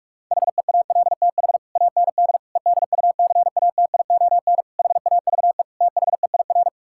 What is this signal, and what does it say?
Morse code, heard as a keyed tone, about 46 wpm, 700 Hz. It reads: HEALTH AND EDUCATION HAVE THEIR
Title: Sentences from Top 1000 Words - Review Part 6 - 45wpm